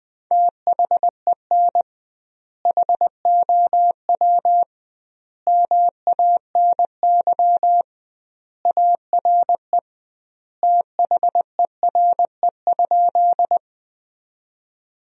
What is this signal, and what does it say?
Morse code, heard as a keyed tone, about 21 wpm, 700 Hz. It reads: THEN HOW MANY ARE THERE?
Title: 2x Word-Spacing - Sentences from Top 100 Words - 20wpm